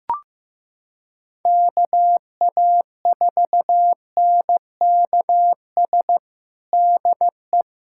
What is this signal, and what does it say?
Morse code, heard as a keyed tone, about 15 wpm, 700 Hz. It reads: KA4NKS DE
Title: USA Callsign DE Callsign 15wpm